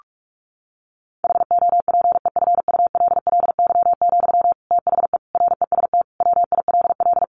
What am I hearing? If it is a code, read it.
Morse code, heard as a keyed tone, about 44 wpm, 700 Hz. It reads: HOPEFULLY, THE REST WILL
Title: Sentences from Top 1500 Words 45wpm